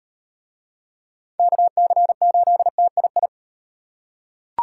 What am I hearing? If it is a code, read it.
Morse code, heard as a keyed tone, about 38 wpm, 700 Hz. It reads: KC8TII